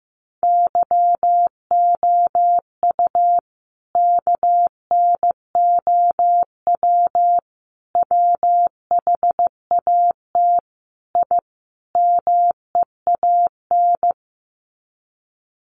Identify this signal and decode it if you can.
Morse code, heard as a keyed tone, about 15 wpm, 700 Hz. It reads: YOU KNOW WHAT I MEAN
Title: Sets of 5 Words 15wpm